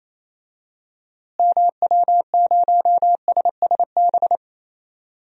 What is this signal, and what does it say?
Morse code, heard as a keyed tone, about 28 wpm, 700 Hz. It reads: MW0SSB
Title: DXCC-100 Call Signs - 6 Characters - Repeated 3x - 28wpm